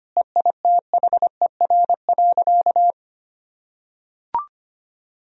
Morse code, heard as a keyed tone, about 25 wpm, 700 Hz. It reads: EITHER.
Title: Sentences from Top 1000 Words - Review Part 4 - 25wpm